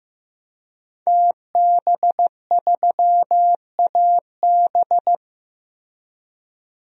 Morse code, heard as a keyed tone, about 15 wpm, 700 Hz. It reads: TB3AB
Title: DXCC-100 Call Signs - 5 Characters - Repeated 3x - 15wpm